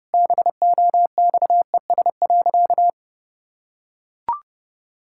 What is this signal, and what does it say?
Morse code, heard as a keyed tone, about 32 wpm, 700 Hz. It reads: BOXES.
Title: Pangram Challenge v2 - 30wpm